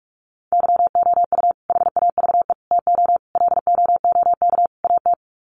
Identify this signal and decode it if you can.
Morse code, heard as a keyed tone, about 44 wpm, 700 Hz. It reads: YOU HAVE TO LOOK AT
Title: Sets of 5 Words Encore 45wpm